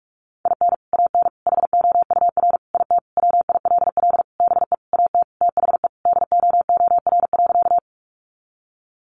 Morse code, heard as a keyed tone, about 44 wpm, 700 Hz. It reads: IN AN HOUR IT WILL BE AT THE DOOR.
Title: Sentences from Top 400 Words 45wpm